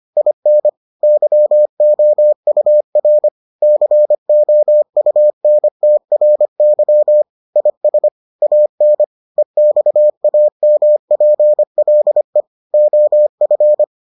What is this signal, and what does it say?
Morse code, heard as a keyed tone, about 25 wpm, 600 Hz. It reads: IN YOUR COUNTRY IS AN EXAMPLE OF